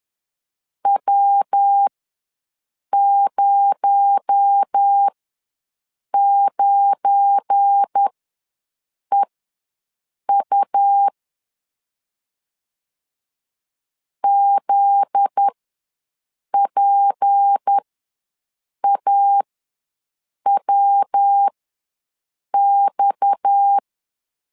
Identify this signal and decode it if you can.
Morse code, heard as a keyed tone, about 10 wpm, spaced as for 6 wpm, 800 Hz. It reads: W09EU ZPAWX
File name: LEZIONI / 7 GF9Z / 7 GF9Z ws 10wpm 3919